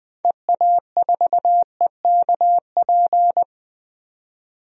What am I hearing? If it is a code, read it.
Morse code, heard as a keyed tone, about 20 wpm, 700 Hz. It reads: EA4EKP
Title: DXCC-100 Call Signs - 6 Characters - 20wpm